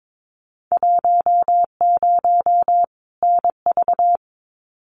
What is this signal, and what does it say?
Morse code, heard as a keyed tone, about 22 wpm, 700 Hz. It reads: 10 NV
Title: Winter Field Day Exchanges 22wpm